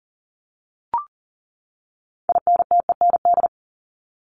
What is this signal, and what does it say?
Morse code, heard as a keyed tone, about 40 wpm, 700 Hz. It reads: INTEND
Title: ED Suffix - Top 100 Words with Base Words 40wpm